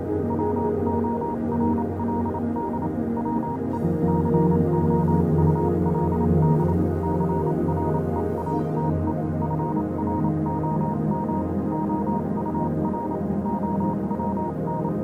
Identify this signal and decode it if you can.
Morse code, heard as a keyed tone, about 28 wpm, spaced as for 25 wpm, 940 Hz, with noise. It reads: WORLD WE COULD CREATE FROM OUR OWN